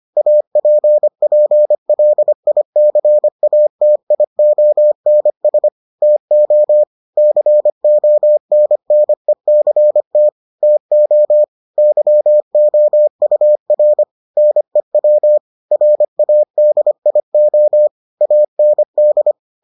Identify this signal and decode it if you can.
Morse code, heard as a keyed tone, about 25 wpm, 600 Hz. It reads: APPLICATIONS TO CONNECT TO YOUR NEW RADIO AND